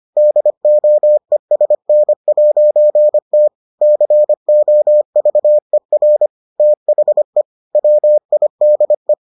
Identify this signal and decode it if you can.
Morse code, heard as a keyed tone, about 25 wpm, 600 Hz. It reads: DOESN'T COVER THE WIDE